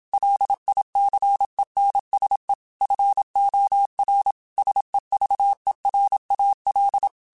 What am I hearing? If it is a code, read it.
Morse code, heard as a keyed tone, about 27 wpm, 800 Hz. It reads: LICENSE FOR SEVERAL
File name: QSOs / QSOs-mp3s / WA7QOX 2@25